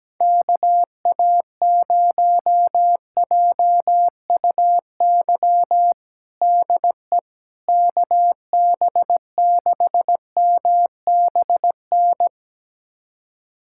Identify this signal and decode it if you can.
Morse code, heard as a keyed tone, about 17 wpm, 700 Hz. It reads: KA0JUY DE KB6MBN
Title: USA Callsign DE Callsign 17wpm